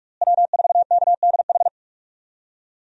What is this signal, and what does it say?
Morse code, heard as a keyed tone, about 46 wpm, 700 Hz. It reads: W4KDH
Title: USA Call Signs - 5 Characters - Repeated 2x - 45wpm